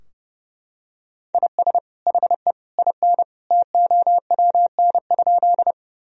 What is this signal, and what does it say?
Morse code, heard as a keyed tone, about 30 wpm, 700 Hz. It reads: IS HE IN TOWN?
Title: Sentences from Top 1000 Words - Review Part 2 - 30wpm